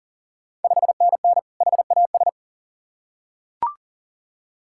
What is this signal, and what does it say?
Morse code, heard as a keyed tone, about 38 wpm, 690 Hz. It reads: 5NN HAS
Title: Ontario QSO Party Exchange 40wpm